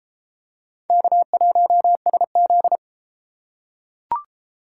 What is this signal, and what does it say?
Morse code, heard as a keyed tone, about 30 wpm, 700 Hz. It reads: K1SZ